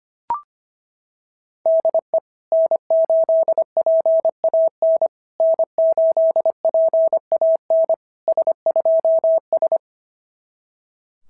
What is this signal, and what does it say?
Morse code, heard as a keyed tone, about 24 wpm, 650 Hz. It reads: DE N8PAN N8PAN S2S
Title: SOTA QSO Practice 25wpm